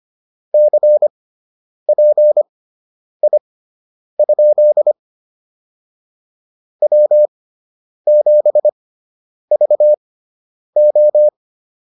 Morse code, heard as a keyed tone, about 25 wpm, spaced as for 9 wpm, 600 Hz. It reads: CPI? W7VO
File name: QSO 105 09